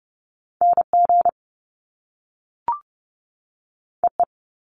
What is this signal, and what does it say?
Morse code, heard as a keyed tone, about 30 wpm, 700 Hz. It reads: NG EE
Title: Top Two Letter Combinations in Words 30wpm